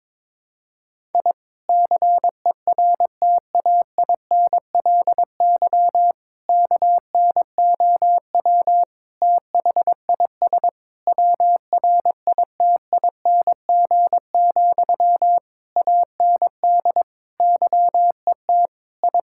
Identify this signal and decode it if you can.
Morse code, heard as a keyed tone, about 22 wpm, 700 Hz. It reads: I CERTAINLY KNOW THIS WRITING, AND YET I
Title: Sentences from Top 1000 Words - Review Part 6 - 22wpm